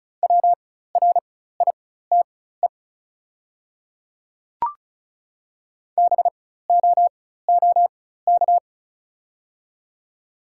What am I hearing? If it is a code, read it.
Morse code, heard as a keyed tone, about 36 wpm, spaced as for 16 wpm, 690 Hz. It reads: WRITE BOOK
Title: 4x Character Spacing - Top 300 Words - 35wpm